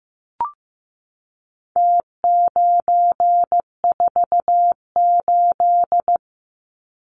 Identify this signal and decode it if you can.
Morse code, heard as a keyed tone, about 15 wpm, 700 Hz. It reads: T948
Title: Numbers - Contest Style - 4-Digit 15wpm